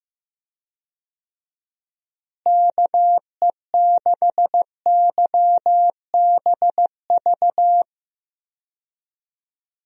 Morse code, heard as a keyed tone, about 15 wpm, 700 Hz. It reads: KE6YBV
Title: USA Call Signs - 6 Characters - Repeated 3x - 15wpm